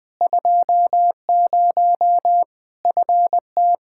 Morse code, heard as a keyed tone, about 20 wpm, 700 Hz. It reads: 20 FT
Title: K1USN SST QSO Practice 20wpm